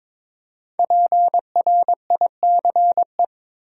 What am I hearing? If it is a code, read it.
Morse code, heard as a keyed tone, about 22 wpm, 700 Hz. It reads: PRICE
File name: 5 Letter Words - First 3 Letters Given - 22wpm